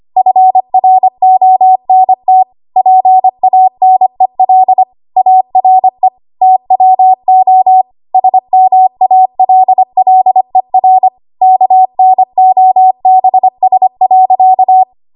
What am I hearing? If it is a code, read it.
Morse code, heard as a keyed tone, about 25 wpm, 750 Hz. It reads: FRONT PANEL ARE TWO SMALLER KNOBS.
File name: Archive / 25 WPM / 25WPM